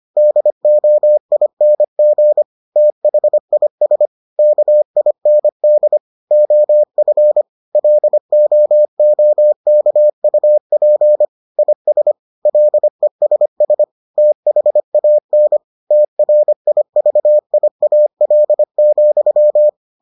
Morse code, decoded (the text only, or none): DOING THIS KIND OF LOOKUP IS LESS THAN TRIVIAL,